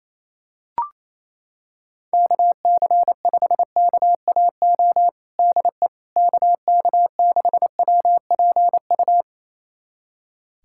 Morse code, heard as a keyed tone, about 28 wpm, 700 Hz. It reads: KC5KAO DE KK6WPU
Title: USA Callsign DE Callsign 28wpm